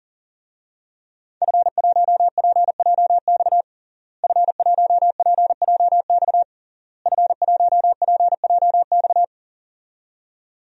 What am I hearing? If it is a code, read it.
Morse code, heard as a keyed tone, about 40 wpm, 700 Hz. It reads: F1PJX F1PJX F1PJX
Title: DXCC-100 Call Signs - 5 Characters - Repeated 3x - 40wpm